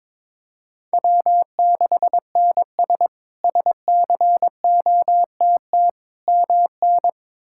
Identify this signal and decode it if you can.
Morse code, heard as a keyed tone, about 22 wpm, 700 Hz. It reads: W6NS SCOTT MN
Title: K1USN SST QSO Practice - 22wpm